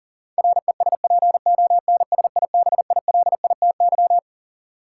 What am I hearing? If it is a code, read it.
Morse code, heard as a keyed tone, about 40 wpm, 700 Hz. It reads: RESPONSIBILITY
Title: Sentences from Top 1200 Words - New Word Drill - 40wpm